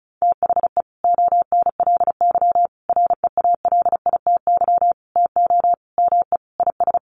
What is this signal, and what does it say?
Morse code, heard as a keyed tone, about 34 wpm, 700 Hz. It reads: THE ONLY REALITY TO ME IS
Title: Sentences from Top 1200 Words 35wpm